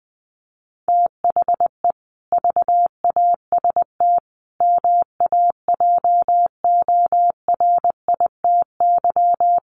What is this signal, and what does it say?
Morse code, heard as a keyed tone, about 20 wpm, 700 Hz. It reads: THE VAST MAJORITY
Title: Sets of 5 Words Encore 20wpm